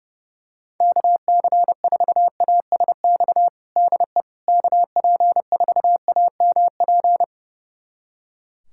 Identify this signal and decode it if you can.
Morse code, heard as a keyed tone, about 32 wpm, 700 Hz. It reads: KC4ASX DE KP4AMP
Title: USA Callsign DE Callsign 30wpm